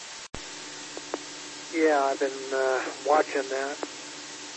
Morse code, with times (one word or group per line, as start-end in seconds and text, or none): none